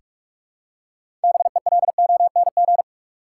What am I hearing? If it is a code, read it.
Morse code, heard as a keyed tone, about 46 wpm, 700 Hz. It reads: BELONG
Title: Six Letter Words - 45wpm